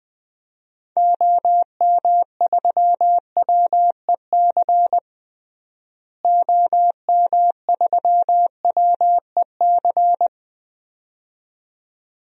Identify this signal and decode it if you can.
Morse code, heard as a keyed tone, about 20 wpm, 690 Hz. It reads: OM3WEC OM3WEC
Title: DXCC-100 Call Signs - 6 Characters - Repeated 2x - 20wpm